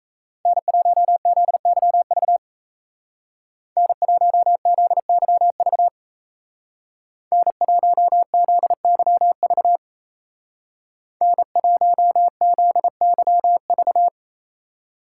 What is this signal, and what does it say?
Morse code, heard as a keyed tone, about 35 wpm, 700 Hz. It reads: N1ZYV N1ZYV N1ZYV N1ZYV
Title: Speed Racing - USA Call Signs 28wpm